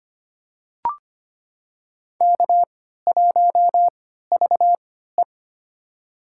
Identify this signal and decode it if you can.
Morse code, heard as a keyed tone, about 25 wpm, spaced as for 14 wpm, 700 Hz. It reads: K1VE
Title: USA Call Signs - 4 Characters - 3x Character Spacing - 25wpm